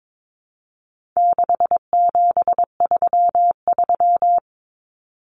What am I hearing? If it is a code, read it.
Morse code, heard as a keyed tone, about 22 wpm, 700 Hz. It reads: 6733
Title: Numbers - 4-Digit 22wpm